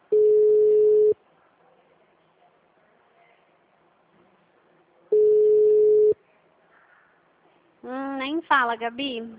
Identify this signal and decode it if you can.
Ringback tone (Europe, 425 Hz), with speech.
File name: out- -2015- - - .3984